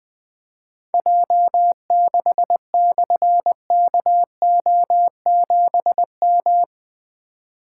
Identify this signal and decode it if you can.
Morse code, heard as a keyed tone, about 21 wpm, 700 Hz. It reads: J6/KO7M
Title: Difficult Call Signs 20wpm